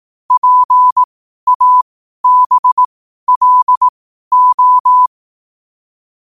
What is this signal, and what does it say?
Morse code, heard as a keyed tone, about 18 wpm, spaced as for 12 wpm, 1000 Hz. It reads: PABLO
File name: qotdfiles / 12wpm